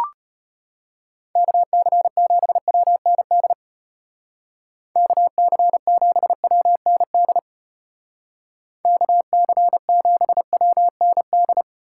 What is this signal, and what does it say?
Morse code, heard as a keyed tone, about 31 wpm, 700 Hz. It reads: KC7WND KC7WND KC7WND